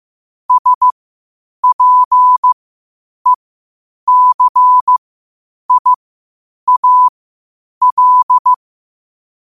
Morse code, heard as a keyed tone, about 15 wpm, spaced as for 8 wpm, 1000 Hz. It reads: SPECIAL
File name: qotdfiles / 8wpm